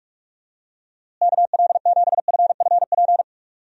Morse code, heard as a keyed tone, about 44 wpm, 700 Hz. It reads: KL7FFP
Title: USA Call Signs - 6 Characters - Repeated 3x - 45wpm